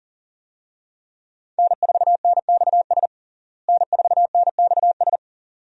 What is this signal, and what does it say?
Morse code, heard as a keyed tone, about 40 wpm, 700 Hz. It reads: N4NXS N4NXS
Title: USA Call Signs - 5 Characters - Repeated 2x - 40wpm